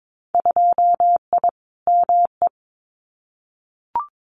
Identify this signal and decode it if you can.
Morse code, heard as a keyed tone, about 22 wpm, 700 Hz. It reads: 2I ME
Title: Winter Field Day Exchanges Encore 22wpm